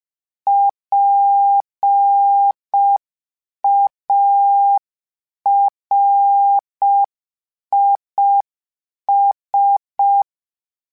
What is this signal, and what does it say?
Morse code, heard as a keyed tone, about 5 wpm, 800 Hz. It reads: PARIS
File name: PARIS Calibration / PARIS mp3s / PARIS@5